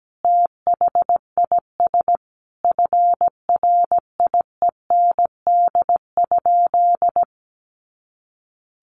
Morse code, heard as a keyed tone, about 17 wpm, 700 Hz. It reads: THIS FRIEND?
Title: Sentences from Top 300 Words 17wpm